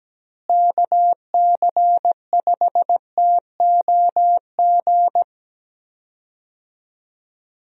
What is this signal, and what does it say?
Morse code, heard as a keyed tone, about 17 wpm, 700 Hz. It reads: KC5TOG
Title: USA Call Signs - 6 Characters - Repeated 2x - 17wpm